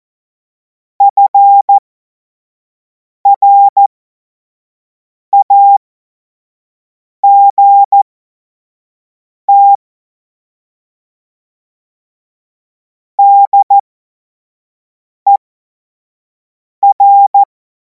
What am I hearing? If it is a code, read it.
Morse code, heard as a keyed tone, about 14 wpm, spaced as for 5 wpm, 800 Hz. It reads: FRAGT DER
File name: Morsekurs / Witze / Witz 17. März 2026 25 F70